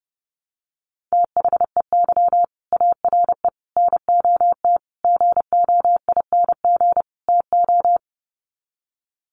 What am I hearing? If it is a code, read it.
Morse code, heard as a keyed tone, about 30 wpm, 700 Hz. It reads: THEY ARE NOT GOING TO
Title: Sets of 5 Words Encore2 30wpm